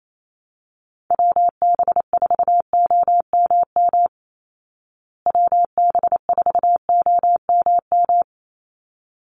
Morse code, heard as a keyed tone, about 28 wpm, 700 Hz. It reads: WB4OMM WB4OMM
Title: Call Signs Repeated 3x - 28wpm